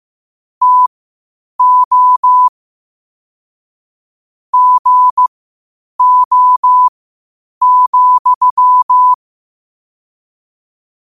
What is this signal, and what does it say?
Morse code, heard as a keyed tone, about 15 wpm, spaced as for 8 wpm, 1000 Hz. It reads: TO GO,
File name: qotdfiles / 8wpm